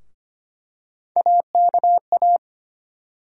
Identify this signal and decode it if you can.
Morse code, heard as a keyed tone, about 25 wpm, 700 Hz. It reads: AKA
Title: USA Call Sign Postfixes - 3 Letters 25wpm